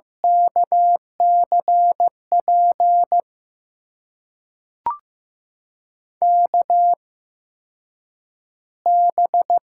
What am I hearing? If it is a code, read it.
Morse code, heard as a keyed tone, about 16 wpm, spaced as for 8 wpm, 700 Hz. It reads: KCP KB
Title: USA Call Signs - 6 Characters - 8x Character Spacing - 15wpm